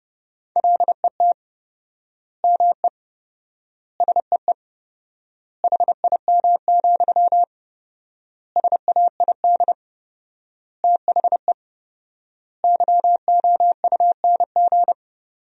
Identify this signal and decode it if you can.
Morse code, heard as a keyed tone, about 31 wpm, 700 Hz. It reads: LET ME SEE HIM, SAID THE YOUNG
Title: 4x Word-Spacing - Sentences from Top 400 Words - 30wpm